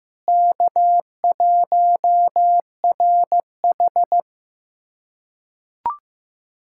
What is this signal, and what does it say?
Morse code, heard as a keyed tone, about 15 wpm, 700 Hz. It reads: K1RH
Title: USA Callsign DE Callsign - 4 Characters - 15wpm